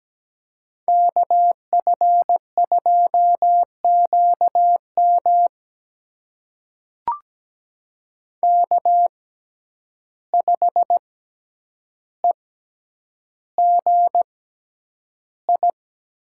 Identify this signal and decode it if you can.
Morse code, heard as a keyed tone, about 17 wpm, spaced as for 9 wpm, 700 Hz. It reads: KF2QM K5EGI
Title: USA Call Signs - 5 Characters - 6x Character Spacing - 17wpm